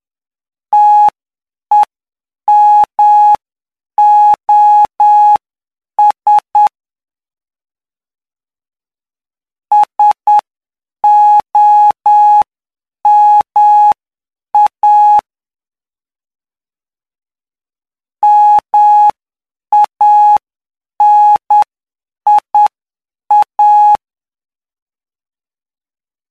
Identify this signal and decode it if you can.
Morse code, heard as a keyed tone, about 9 wpm, spaced as for 7 wpm, 840 Hz. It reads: TEMOS SOMA MANIA
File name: cw / py2lcd / 06-PY2LCD-Li��o 15